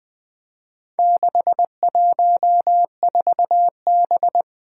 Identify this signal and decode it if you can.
Morse code, heard as a keyed tone, about 20 wpm, 700 Hz. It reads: 614B
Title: Four Challenging Letter-Number Combinations 20wpm